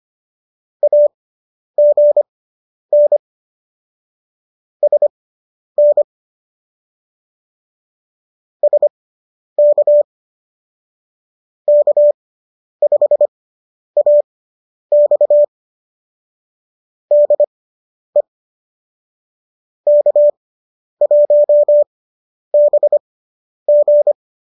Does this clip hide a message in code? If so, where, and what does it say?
Yes: Morse code from 0.8 to 24.1 s, heard as a keyed tone, about 25 wpm, spaced as for 10 wpm, 590 Hz. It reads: AGN SN SK K5AX DE K1BG